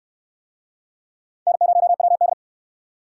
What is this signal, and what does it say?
Morse code, heard as a keyed tone, about 62 wpm, 700 Hz. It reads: N0WD